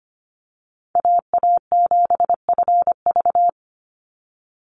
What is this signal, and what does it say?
Morse code, heard as a keyed tone, about 25 wpm, 700 Hz. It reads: AA7FV
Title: Call Signs - 25wpm